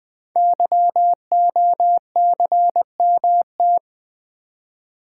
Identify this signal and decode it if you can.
Morse code, heard as a keyed tone, about 21 wpm, 700 Hz. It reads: YOCMT
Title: Code Groups - 5 Letters - 20wpm